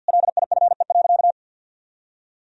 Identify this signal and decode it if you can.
Morse code, heard as a keyed tone, about 50 wpm, 700 Hz. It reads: LIFE.
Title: Sentences from Top 1000 Words - Review Part 3 - 50wpm